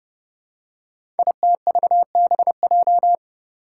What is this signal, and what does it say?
Morse code, heard as a keyed tone, about 28 wpm, 700 Hz. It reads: ITVBJ
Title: Code Groups - 5 Letters - 30wpm